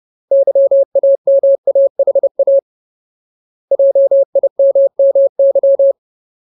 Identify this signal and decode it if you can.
Morse code, heard as a keyed tone, about 30 wpm, 550 Hz. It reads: YAMAHA JIMMY
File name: Ultimate Word List / 30WPM Ultimate